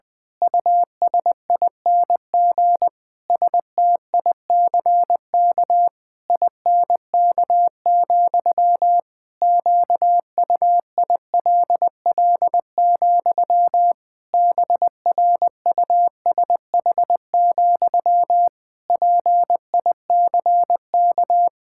Morse code, heard as a keyed tone, about 20 wpm, 700 Hz. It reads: USING STICK INK, QUILL, BRUSH, PICK
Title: Pangram Challenge v2 - 20wpm